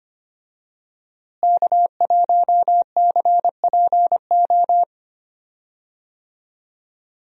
Morse code, heard as a keyed tone, about 25 wpm, 700 Hz. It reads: K1CPO
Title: USA Call Signs - 5 Characters - Repeated 2x - 25wpm